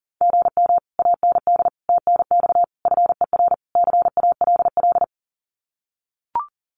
Morse code, heard as a keyed tone, about 40 wpm, 700 Hz. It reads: GM AND TNX FER CALL
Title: Common QSO Phrases Encore 40wpm